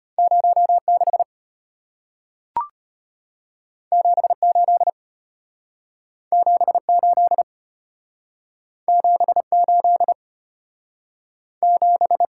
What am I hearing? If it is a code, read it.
Morse code, heard as a keyed tone, about 34 wpm, 700 Hz. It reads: 06 78 78 78 7